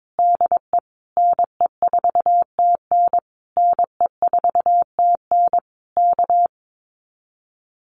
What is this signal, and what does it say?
Morse code, heard as a keyed tone, about 22 wpm, 700 Hz. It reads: DE NE4TN NE4TN K
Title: Common QSO Phrases Encore 22wpm